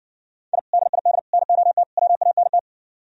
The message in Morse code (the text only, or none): I DID NOT WANT